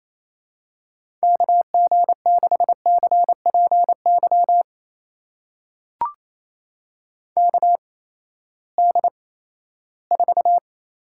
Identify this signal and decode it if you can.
Morse code, heard as a keyed tone, about 28 wpm, spaced as for 16 wpm, 700 Hz. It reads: KG6CPY KD4
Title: USA Call Signs - 6 Characters - 8x Character Spacing - 28wpm